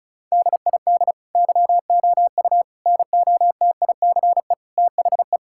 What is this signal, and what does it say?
Morse code, heard as a keyed tone, about 36 wpm, 700 Hz. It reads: DID YOU NOTICE THE
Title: Sentences from Top 1000 Words - Review Part 5 - 35wpm